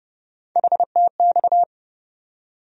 Morse code, heard as a keyed tone, about 30 wpm, 700 Hz. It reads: HTX
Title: USA Call Sign Postfixes - 3 Letters 30wpm